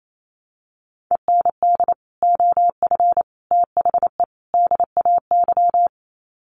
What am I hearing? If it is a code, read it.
Morse code, heard as a keyed tone, about 28 wpm, 700 Hz. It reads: END OF THE DAY
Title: Sets of 4 Words 28wpm